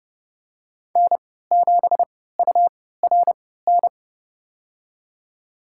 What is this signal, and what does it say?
Morse code, heard as a keyed tone, about 32 wpm, spaced as for 17 wpm, 700 Hz. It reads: N7URN
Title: USA Call Signs - 5 Characters - 3x Character Spacing - 30wpm